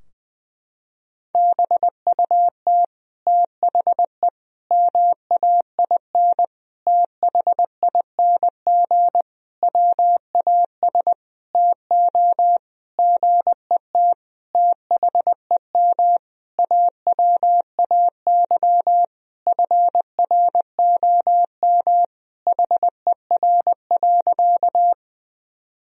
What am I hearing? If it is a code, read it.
Morse code, heard as a keyed tone, about 21 wpm, 700 Hz. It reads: BUT THE MAIN THING WAS TO GET THEM AWAY FROM HER.
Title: Sentences from Top 1000 Words - Review Part 2 - 20wpm